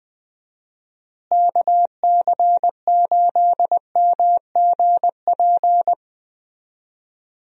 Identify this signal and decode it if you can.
Morse code, heard as a keyed tone, about 20 wpm, 700 Hz. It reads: KC8MGP
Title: USA Call Signs - 6 Characters - Repeated 2x - 20wpm